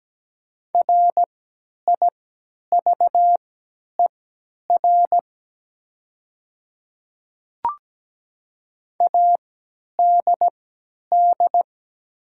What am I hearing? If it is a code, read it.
Morse code, heard as a keyed tone, about 17 wpm, spaced as for 10 wpm, 700 Hz. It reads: RIVER ADD
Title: 3x Character Spacing - Top 300 Words - 17wpm